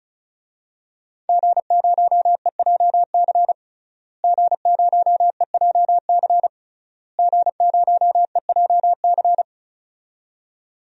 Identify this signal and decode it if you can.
Morse code, heard as a keyed tone, about 34 wpm, 700 Hz. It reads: G0EJC G0EJC G0EJC
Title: DXCC-100 Call Signs - 5 Characters - Repeated 3x - 35wpm